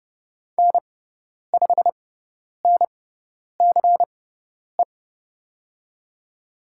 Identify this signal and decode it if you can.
Morse code, heard as a keyed tone, about 30 wpm, spaced as for 10 wpm, 700 Hz. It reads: N5NCE
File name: Lesson 16g - Letter c - Call Signs 30wpm